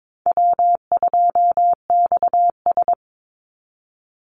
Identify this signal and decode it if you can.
Morse code, heard as a keyed tone, about 22 wpm, 700 Hz. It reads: W2XS
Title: Easy Call Signs - 22wpm